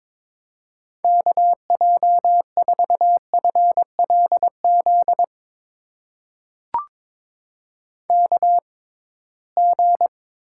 Morse code, heard as a keyed tone, about 22 wpm, spaced as for 16 wpm, 700 Hz. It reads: KJ4FLZ KG
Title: USA Call Signs - 6 Characters - 6x Character Spacing - 22wpm